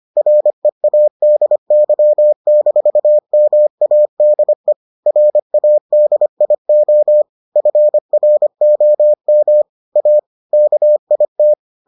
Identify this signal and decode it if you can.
Morse code, heard as a keyed tone, about 25 wpm, 600 Hz. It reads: READY-MADE RADIO FROM A KIT